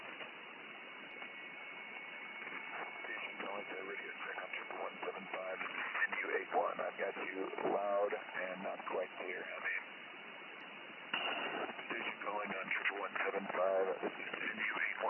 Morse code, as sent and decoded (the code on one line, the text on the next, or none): none